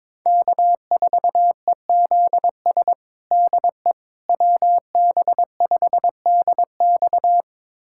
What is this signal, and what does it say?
Morse code, heard as a keyed tone, about 21 wpm, 700 Hz. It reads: K4EZS DE WB5DX
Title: USA Callsign DE Callsign - 5 Characters - 22wpm